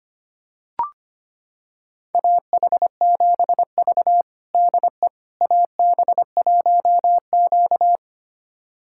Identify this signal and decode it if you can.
Morse code, heard as a keyed tone, about 25 wpm, 700 Hz. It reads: AH7V DE AB1Q
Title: USA Callsign DE Callsign - 4 Characters - 25wpm